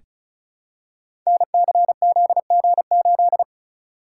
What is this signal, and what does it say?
Morse code, heard as a keyed tone, about 36 wpm, 700 Hz. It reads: NCZG8
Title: Code Groups - 5 Letter:Numbers - 35wpm